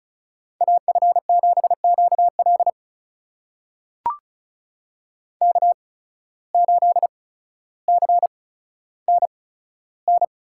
Morse code, heard as a keyed tone, about 36 wpm, spaced as for 15 wpm, 700 Hz. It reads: AF7QL K8CNN
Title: USA Call Signs - 5 Characters - 8x Character Spacing - 35wpm